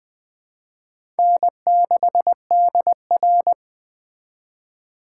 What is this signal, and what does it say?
Morse code, heard as a keyed tone, about 21 wpm, 700 Hz. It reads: N6DR
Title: Speed Racing - USA 1x2 Call Signs 17wpm